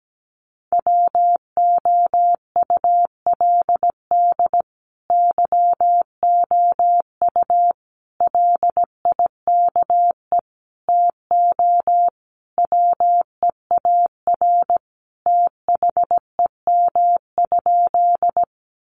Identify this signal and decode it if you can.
Morse code, heard as a keyed tone, about 17 wpm, 700 Hz. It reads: WOULD YOU LIKE TO WEAR THEM?
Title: Sentences from Top 1500 Words 17wpm